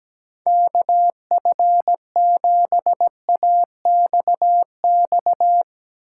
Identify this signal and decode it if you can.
Morse code, heard as a keyed tone, about 17 wpm, 700 Hz. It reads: KF7AXX
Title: USA Call Signs - 6 Characters - 4x Character Spacing - 17wpm